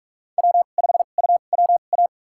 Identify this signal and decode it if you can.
Morse code, heard as a keyed tone, about 45 wpm, spaced as for 32 wpm, 700 Hz. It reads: W5UWA
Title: USA Call Signs - 2x Character Spacing - 45wpm